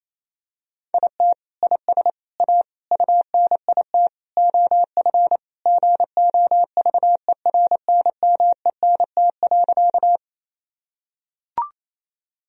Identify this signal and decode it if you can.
Morse code, heard as a keyed tone, about 27 wpm, 700 Hz. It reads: IT IS A UNIT OF GOVERNMENT.
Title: Sentences from Top 1000 Words - Review Part 2 - 28wpm